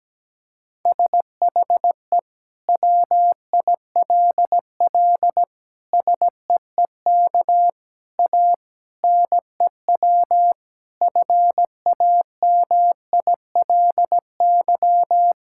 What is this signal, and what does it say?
Morse code, heard as a keyed tone, about 17 wpm, 700 Hz. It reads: SHE WILL SEEK A NEW FAMILY
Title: Sentences from Top 1600 Words 17wpm